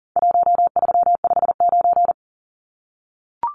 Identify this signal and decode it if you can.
Morse code, heard as a keyed tone, about 40 wpm, 700 Hz. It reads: 1359
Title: Numbers - Contest Style - 4-Digit 40wpm